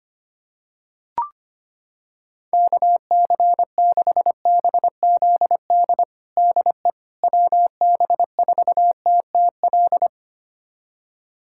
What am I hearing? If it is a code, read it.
Morse code, heard as a keyed tone, about 25 wpm, 700 Hz. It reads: KC6BZD DE WB4TTL
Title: USA Callsign DE Callsign - 6 Characters - 25wpm